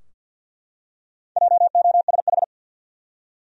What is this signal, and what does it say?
Morse code, heard as a keyed tone, about 49 wpm, 700 Hz. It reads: JOSH
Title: USA Common Names - 50wpm